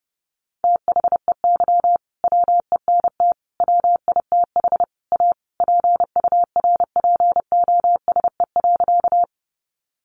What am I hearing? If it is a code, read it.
Morse code, heard as a keyed tone, about 30 wpm, 700 Hz. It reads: THEY WENT WITH A PURPOSE.
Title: Sentences from Top 800 Words 30wpm